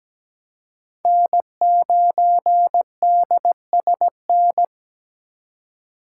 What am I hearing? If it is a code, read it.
Morse code, heard as a keyed tone, about 17 wpm, 700 Hz. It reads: N9DSN
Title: USA Call Signs - 3x Character Spacing - 17wpm